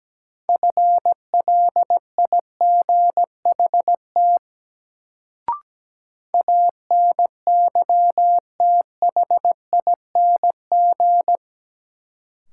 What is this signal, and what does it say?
Morse code, heard as a keyed tone, about 17 wpm, 700 Hz. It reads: FLIGHT ANYTHING
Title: Sentences from Top 1000 Words - New Word Drill - 17wpm